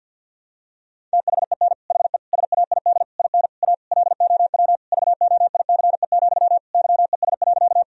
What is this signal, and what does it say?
Morse code, heard as a keyed tone, about 52 wpm, 700 Hz. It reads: THEN HE SAID IN A LOW VOICE, YES.
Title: Sentences from Top 1000 Words - Review Part 1 - 50wpm